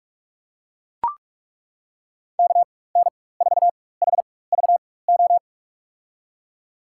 Morse code, heard as a keyed tone, about 43 wpm, spaced as for 21 wpm, 700 Hz. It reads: KN4HVO
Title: USA Call Signs - 6 Characters - 4x Character Spacing - 45wpm